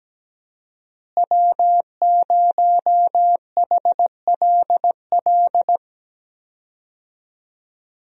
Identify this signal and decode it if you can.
Morse code, heard as a keyed tone, about 17 wpm, 700 Hz. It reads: W0HLL
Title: USA Call Signs - 5 Characters - Repeated 2x - 17wpm